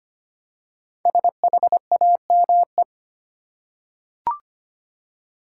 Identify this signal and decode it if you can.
Morse code, heard as a keyed tone, about 25 wpm, 700 Hz. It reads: SHAME
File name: 5 Letter Words - First 3 Letters Given - 25wpm